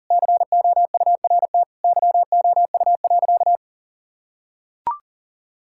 Morse code, heard as a keyed tone, about 40 wpm, 700 Hz. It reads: COURT YOU.
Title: Sentences from Top 1600 Words 40wpm